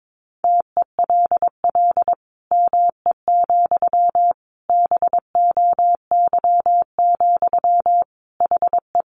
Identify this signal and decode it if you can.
Morse code, heard as a keyed tone, about 22 wpm, 700 Hz. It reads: TELL ME, BOY, HE
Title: Sentences from Top 1500 Words 22wpm